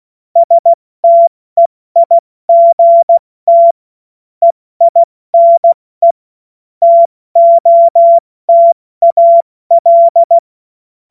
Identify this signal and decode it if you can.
Morse code, heard as a keyed tone, about 16 wpm, 680 Hz. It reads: STEIGT EINE TOTAL
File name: Morsekurs / Witze / Witz 31. August 2021, 70 F80